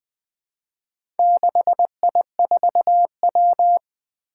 Morse code, heard as a keyed tone, about 20 wpm, 700 Hz. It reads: 6I4W
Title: Four Challenging Letter-Number Combinations 20wpm